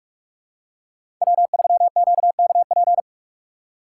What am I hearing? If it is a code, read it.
Morse code, heard as a keyed tone, about 46 wpm, 700 Hz. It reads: W3QKP
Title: USA Call Signs - 3x Character Spacing - 45wpm